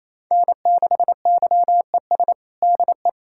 Morse code, heard as a keyed tone, about 28 wpm, 700 Hz. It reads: N6YES DE
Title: Practice QSO - Exchange 4 - Ending - 28wpm